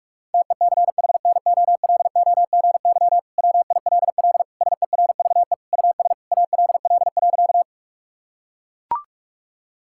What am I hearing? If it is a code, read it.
Morse code, heard as a keyed tone, about 41 wpm, 700 Hz. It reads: TECHNOLOGY WILL SERVE US ALL.
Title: Sentences from Top 1000 Words - Review Part 5 - 45wpm